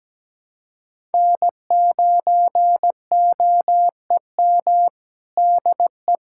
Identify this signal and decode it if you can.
Morse code, heard as a keyed tone, about 17 wpm, 700 Hz. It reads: N9OEM DE
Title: USA Callsign DE Callsign - 5 Characters - 17wpm